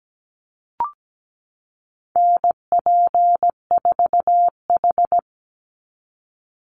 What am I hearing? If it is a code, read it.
Morse code, heard as a keyed tone, about 17 wpm, 700 Hz. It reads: NP4H
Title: Call Signs Repeated 2x - 17wpm